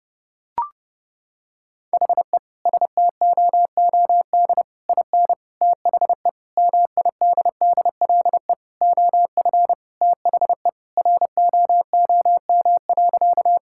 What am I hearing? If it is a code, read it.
Morse code, heard as a keyed tone, about 32 wpm, 700 Hz. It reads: HE STOOD IN THE MIDDLE OF THE ROOM.
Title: Sentences from Top 1000 Words - Review Part 6 - 30wpm